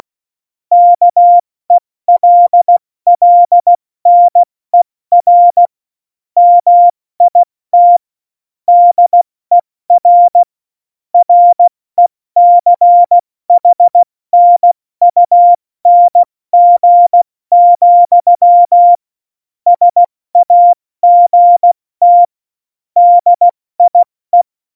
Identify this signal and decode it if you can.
Morse code, heard as a keyed tone, about 16 wpm, 700 Hz. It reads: KELLNER MIT DER RECHNUNG, SAGT DIE
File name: Morsekurs / Witze / Witz 23. April 2019, 70 F80